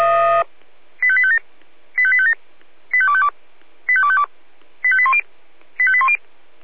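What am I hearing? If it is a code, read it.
Five-tone selective call (ZVEI), heard as quick runs of five tones, tones of about 70 ms. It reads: 85857 85857 85252 85252 85810 85810
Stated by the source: ZVEI five-tone call 85850 received by SDR.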